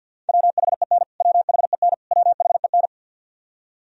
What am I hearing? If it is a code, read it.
Morse code, heard as a keyed tone, about 53 wpm, 700 Hz. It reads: WHEN WHEN WHEN
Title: Top 100 Words Repeated 50wpm